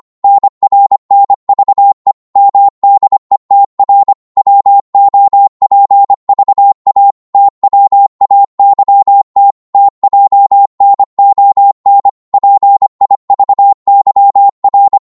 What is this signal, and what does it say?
Morse code, heard as a keyed tone, about 25 wpm, 800 Hz. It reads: NRNVE MDETR WOPVA TWAYT TJNON PIVYR
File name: Morse / ltr-25wpm